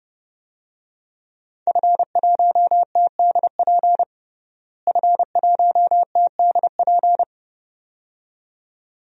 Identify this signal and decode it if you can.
Morse code, heard as a keyed tone, about 32 wpm, 700 Hz. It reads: F1TDP F1TDP
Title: DXCC-100 Call Signs - 5 Characters - Repeated 2x - 30wpm